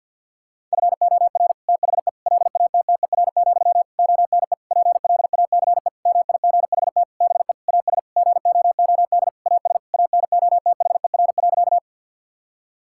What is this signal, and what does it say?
Morse code, heard as a keyed tone, about 50 wpm, 700 Hz. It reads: FOR THE LATTER, ONE PLACE MIGHT BE AS GOOD AS ANOTHER.
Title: Sentences from Top 1600 Words 50wpm